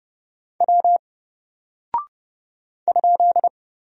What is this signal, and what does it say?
Morse code, heard as a keyed tone, about 30 wpm, 700 Hz. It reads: W ?
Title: Single Character (Letter:Number:Punctuation:Prosign) 30wpm